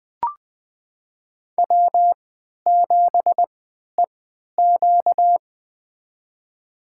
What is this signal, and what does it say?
Morse code, heard as a keyed tone, about 20 wpm, spaced as for 11 wpm, 700 Hz. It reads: W7EQ
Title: USA Call Signs - 4 Characters - 3x Character Spacing - 20wpm